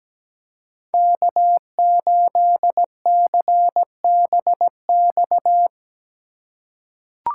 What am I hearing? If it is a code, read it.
Morse code, heard as a keyed tone, about 17 wpm, 700 Hz. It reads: K8CBX
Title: USA Call Signs - 5 Characters - 2x Character Spacing - 17wpm